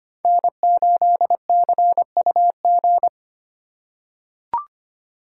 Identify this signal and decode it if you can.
Morse code, heard as a keyed tone, about 25 wpm, 700 Hz. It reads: N8CUG
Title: USA Call Signs - 2x Character Spacing - 25wpm